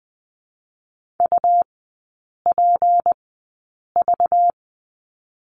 Morse code, heard as a keyed tone, about 20 wpm, spaced as for 8 wpm, 700 Hz. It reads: UPV
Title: 218 Lesson 35e - Number 0 - Three Characters 20wpm